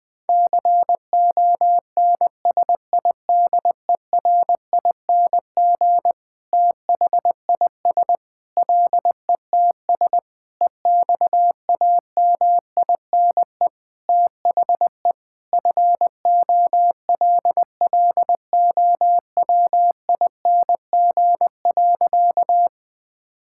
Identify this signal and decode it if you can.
Morse code, heard as a keyed tone, about 21 wpm, 700 Hz. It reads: CONSIDERING THIS LETS EXAMINE THE FOLLOWING.
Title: Sentences with 9-Word Phrases - Spoken Beginning - 20wpm